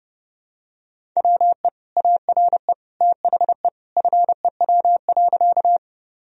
Morse code, heard as a keyed tone, about 30 wpm, 700 Hz. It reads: WE ARE THE FEW.
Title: Sentences from Top 1000 Words - Review Part 1 - 30wpm